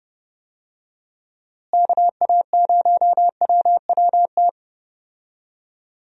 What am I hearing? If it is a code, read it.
Morse code, heard as a keyed tone, about 30 wpm, 700 Hz. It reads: KA0WWT
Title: USA Call Signs - 6 Characters - Repeated 3x - 30wpm